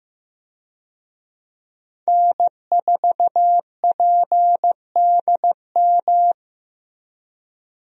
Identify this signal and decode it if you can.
Morse code, heard as a keyed tone, about 15 wpm, 700 Hz. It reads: N4PDM
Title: USA Call Signs - 5 Characters - Repeated 2x - 15wpm